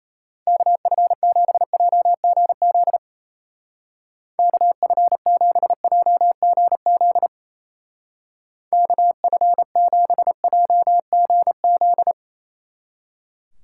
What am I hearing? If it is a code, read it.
Morse code, heard as a keyed tone, about 33 wpm, 700 Hz. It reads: KF7JGZ KF7JGZ KF7JGZ
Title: Speed Racing - USA 2x3 Call Signs 28wpm